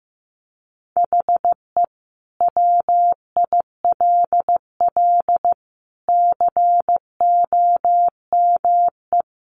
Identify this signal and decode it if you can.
Morse code, heard as a keyed tone, about 15 wpm, 700 Hz. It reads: HE WILL COME
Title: Sentences from Top 1400 Words 15wpm